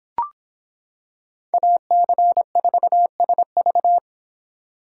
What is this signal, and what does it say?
Morse code, heard as a keyed tone, about 26 wpm, 700 Hz. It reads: AC4SV